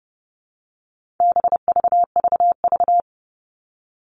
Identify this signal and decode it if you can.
Morse code, heard as a keyed tone, about 30 wpm, 700 Hz. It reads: BVVV
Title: All Cut Numbers - 4-Digit 30wpm